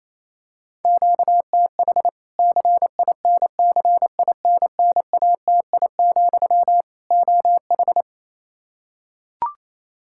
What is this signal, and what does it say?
Morse code, heard as a keyed tone, about 28 wpm, 700 Hz. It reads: QTH CINCINNATI, OH
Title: Practice QSO - Exchange 1 - Beginning - 28wpm